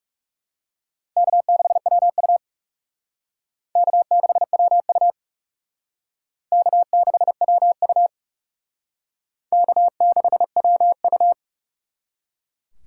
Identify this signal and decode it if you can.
Morse code, heard as a keyed tone, about 37 wpm, 700 Hz. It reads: K6WU K6WU K6WU K6WU
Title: Speed Racing - USA 1x2 Call Signs 30wpm